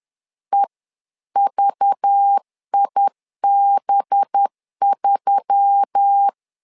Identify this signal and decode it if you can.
Morse code, heard as a keyed tone, about 10 wpm, 800 Hz. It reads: EVIB3
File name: LEZIONI / 5 LBVC / 5 LBV 10wpm 2248